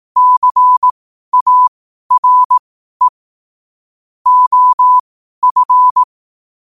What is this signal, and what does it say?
Morse code, heard as a keyed tone, about 18 wpm, spaced as for 12 wpm, 1000 Hz. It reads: CARE OF
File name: qotdfiles / 12wpm